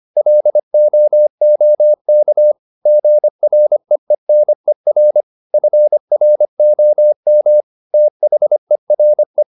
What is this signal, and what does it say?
Morse code, heard as a keyed tone, about 25 wpm, 600 Hz. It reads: LOOK GREENER FROM THERE